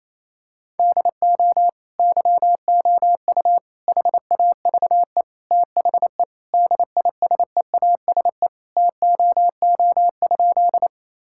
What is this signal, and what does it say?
Morse code, heard as a keyed tone, about 27 wpm, 700 Hz. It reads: DO YOU HAVE THE DISEASE TOO?
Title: Sentences from Top 1000 Words - Review Part 6 - 28wpm